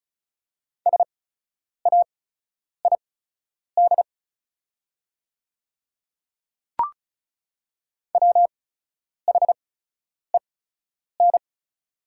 Morse code, heard as a keyed tone, about 36 wpm, spaced as for 10 wpm, 700 Hz. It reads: SAID WHEN
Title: 8x Character Spacing - Top 100 Words - 35wpm